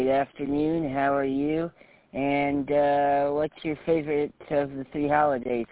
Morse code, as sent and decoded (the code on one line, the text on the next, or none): none